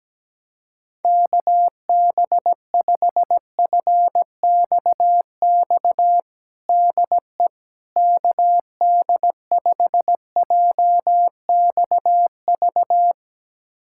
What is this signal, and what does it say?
Morse code, heard as a keyed tone, about 17 wpm, 700 Hz. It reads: KB5FXX DE KD5JXV
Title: USA Callsign DE Callsign - 6 Characters - 17wpm